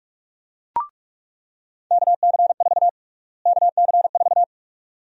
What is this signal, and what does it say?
Morse code, heard as a keyed tone, about 46 wpm, 700 Hz. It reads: KC4 KC4
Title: USA Call Sign Prefixes - Repeated 2x - 45wpm